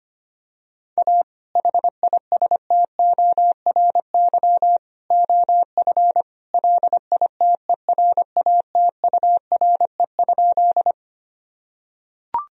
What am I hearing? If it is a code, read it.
Morse code, heard as a keyed tone, about 25 wpm, 700 Hz. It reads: A HISTORY OF LITERATURE?
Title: Sentences from Top 1000 Words - Review Part 3 - 25wpm